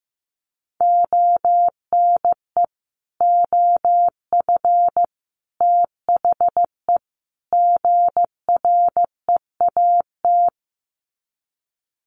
Morse code, heard as a keyed tone, about 15 wpm, 700 Hz. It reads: ONE OF THE GREAT
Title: Sets of 4 Words 15wpm